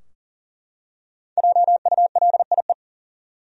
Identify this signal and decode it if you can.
Morse code, heard as a keyed tone, about 40 wpm, 700 Hz. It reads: JULIE
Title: USA Common Names - 40wpm